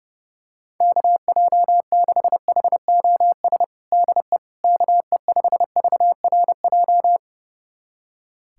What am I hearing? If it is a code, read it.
Morse code, heard as a keyed tone, about 29 wpm, 700 Hz. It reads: KJ6HOS DE KE5VRJ
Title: USA Callsign DE Callsign - 6 Characters - 30wpm